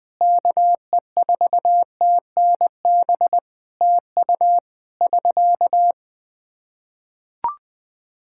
Morse code, heard as a keyed tone, about 20 wpm, 700 Hz. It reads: KE4TNB TU <SK>
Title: K1USN SST QSO Practice 20wpm